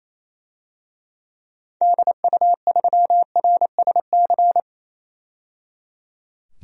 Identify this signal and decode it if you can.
Morse code, heard as a keyed tone, about 28 wpm, 700 Hz. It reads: DU3RSC
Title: DXCC-100 Call Signs - 6 Characters - 28wpm